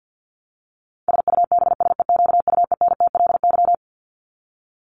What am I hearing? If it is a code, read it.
Morse code, heard as a keyed tone, about 50 wpm, 700 Hz. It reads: SUBSEQUENTLY
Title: LY Suffix - Top 100 Words with Base Words 50wpm